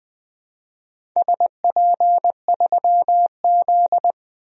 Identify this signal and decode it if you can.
Morse code, heard as a keyed tone, about 20 wpm, 700 Hz. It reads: SP3Z
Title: DXCC-100 Call Signs - 4 Characters - 20wpm